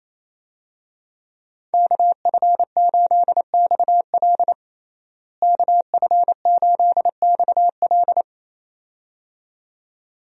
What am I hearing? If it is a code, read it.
Morse code, heard as a keyed tone, about 28 wpm, 700 Hz. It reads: KF8XL KF8XL
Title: USA Call Signs - 5 Characters - Repeated 2x - 28wpm